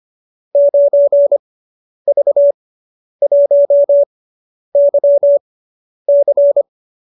Morse code, heard as a keyed tone, about 25 wpm, spaced as for 10 wpm, 570 Hz. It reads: 9V1YC